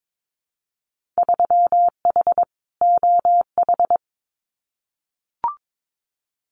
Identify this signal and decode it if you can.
Morse code, heard as a keyed tone, about 22 wpm, 700 Hz. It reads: 3H OH
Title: Winter Field Day Exchanges Encore 22wpm